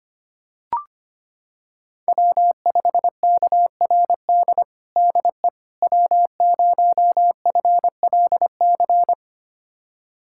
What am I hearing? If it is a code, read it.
Morse code, heard as a keyed tone, about 25 wpm, 700 Hz. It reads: W5KRD DE W0FLC
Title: USA Callsign DE Callsign - 5 Characters - 25wpm